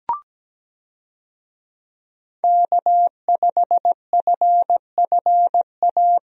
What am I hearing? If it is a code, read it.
Morse code, heard as a keyed tone, about 17 wpm, 700 Hz. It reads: K5FFA
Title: USA Call Signs - 5 Characters - Repeated 3x - 17wpm